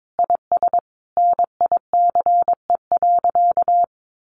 Morse code, heard as a keyed tone, about 22 wpm, 700 Hz. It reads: IS NICE.
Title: Sentences from Top 1100 Words 22wpm